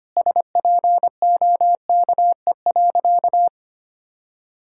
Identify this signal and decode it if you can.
Morse code, heard as a keyed tone, about 25 wpm, 700 Hz. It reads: SPOKE.
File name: practice / 25wpm / Sentences with 4-Word Phrases - 25wpm